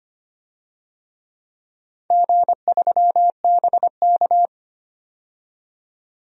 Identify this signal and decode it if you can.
Morse code, heard as a keyed tone, about 25 wpm, 700 Hz. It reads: G3BK
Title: DXCC-100 Call Signs - 4 Characters - 25wpm